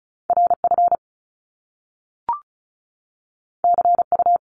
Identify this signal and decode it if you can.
Morse code, heard as a keyed tone, about 36 wpm, 700 Hz. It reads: RF CU
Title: Top Two Letter Combinations in Words 35wpm